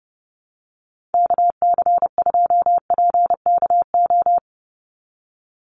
Morse code, heard as a keyed tone, about 30 wpm, 700 Hz. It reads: KC2PKO
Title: USA Call Signs 30wpm